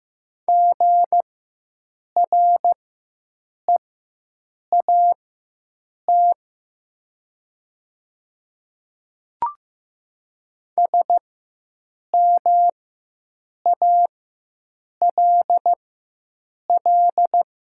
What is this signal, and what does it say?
Morse code, heard as a keyed tone, about 15 wpm, spaced as for 7 wpm, 700 Hz. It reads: GREAT SMALL
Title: 4x Character Spacing - Top 200 Words - 15wpm